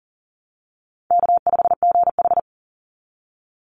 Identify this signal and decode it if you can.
Morse code, heard as a keyed tone, about 40 wpm, 700 Hz. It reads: K5GH
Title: Easy Call Signs - 40wpm